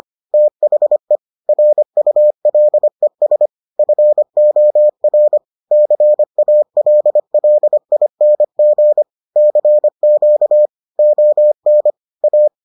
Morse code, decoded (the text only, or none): THE RULES FOR CALLING CQ ON A